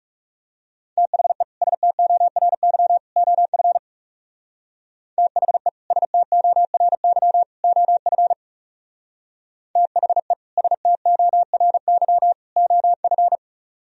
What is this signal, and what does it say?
Morse code, heard as a keyed tone, about 35 wpm, 700 Hz. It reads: THE STORY OF THE STORY OF THE STORY OF
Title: Speed-Racing - Sets of 3 Words 30wpm